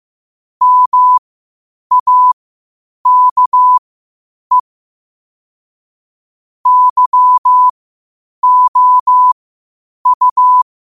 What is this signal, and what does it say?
Morse code, heard as a keyed tone, about 15 wpm, spaced as for 8 wpm, 1000 Hz. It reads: MAKE YOU
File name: qotdfiles / 8wpm